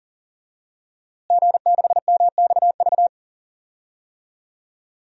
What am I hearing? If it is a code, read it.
Morse code, heard as a keyed tone, about 40 wpm, 700 Hz. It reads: G6MXV
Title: DXCC-100 Call Signs - 5 Characters - Repeated 3x - 40wpm